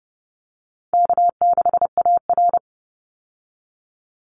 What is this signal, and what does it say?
Morse code, heard as a keyed tone, about 31 wpm, 700 Hz. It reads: K6AR
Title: Call Signs - 30wpm